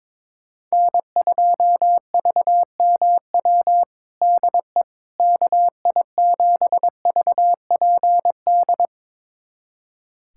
Morse code, heard as a keyed tone, about 22 wpm, 700 Hz. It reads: N2VMW DE KI7VPD
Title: USA Callsign DE Callsign 22wpm